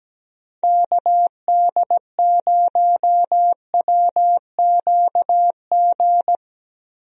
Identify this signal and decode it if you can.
Morse code, heard as a keyed tone, about 17 wpm, 700 Hz. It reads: KD0WQG
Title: USA Call Signs - 6 Characters - 17wpm